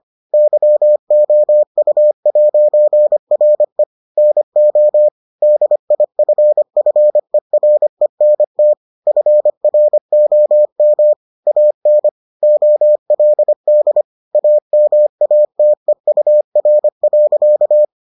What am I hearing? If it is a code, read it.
Morse code, heard as a keyed tone, about 25 wpm, 600 Hz. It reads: YOU'RE NO DIFFERENT FROM AN OLD AMATEUR.